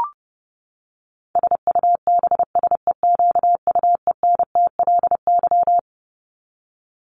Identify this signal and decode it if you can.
Morse code, heard as a keyed tone, about 30 wpm, 700 Hz. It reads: SUBSEQUENTLY
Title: LY Suffix - Top 100 Words with Base Words 30wpm